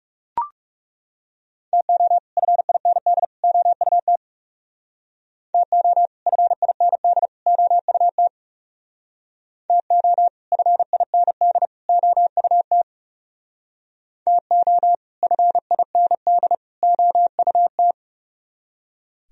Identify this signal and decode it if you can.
Morse code, heard as a keyed tone, about 39 wpm, 700 Hz. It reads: TO FIND OUT TO FIND OUT TO FIND OUT TO FIND OUT